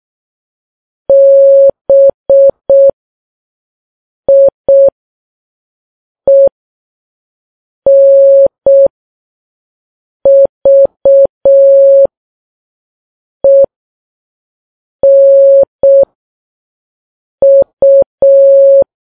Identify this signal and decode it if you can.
Morse code, heard as a keyed tone, about 6 wpm, spaced as for 4 wpm, 550 Hz. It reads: BIENVENU